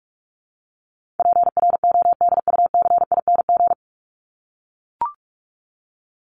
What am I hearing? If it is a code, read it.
Morse code, heard as a keyed tone, about 44 wpm, 700 Hz. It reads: PRODUCING
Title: ING Suffix - Top 100 Words with Base Words 45wpm